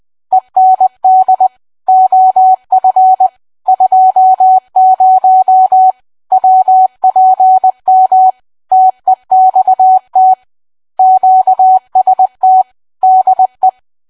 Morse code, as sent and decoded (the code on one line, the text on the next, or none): . -. -.. / --- ..-. / ..--- ----- / .-- .--. -- / - . -..- - / --.- ... - / -.. .
END OF 20 WPM TEXT QST DE